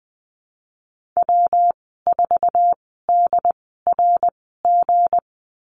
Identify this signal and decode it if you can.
Morse code, heard as a keyed tone, about 20 wpm, spaced as for 14 wpm, 700 Hz. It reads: W4DRG
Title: 161 Lesson 26g - Letter g - Call Signs 20wpm